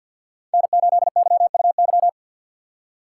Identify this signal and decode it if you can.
Morse code, heard as a keyed tone, about 49 wpm, 700 Hz. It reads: N8YUY
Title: USA Call Signs - 5 Characters - 6x Character Spacing - 50wpm